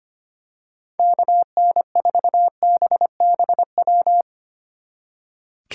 Morse code, heard as a keyed tone, about 27 wpm, 700 Hz. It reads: KN4BBW
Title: Speed Racing - USA 2x3 Call Signs 25wpm